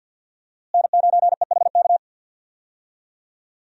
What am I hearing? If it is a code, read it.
Morse code, heard as a keyed tone, about 46 wpm, 700 Hz. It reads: N9EHK
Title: USA Call Signs - 5 Characters - Repeated 3x - 50wpm